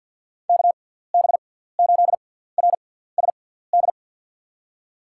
Morse code, heard as a keyed tone, about 47 wpm, 700 Hz. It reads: K B 8 R S D
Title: USA Call Signs - 6x Character Spacing - 50wpm